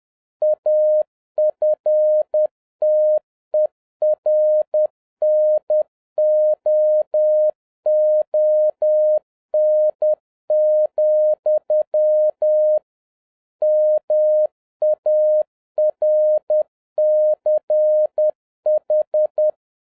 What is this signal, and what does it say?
Morse code, heard as a keyed tone, about 10 wpm, 610 Hz. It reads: AFTERNOON, MARCH
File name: 2011-07-mp3 / n36 THE 3.11 2 10wpm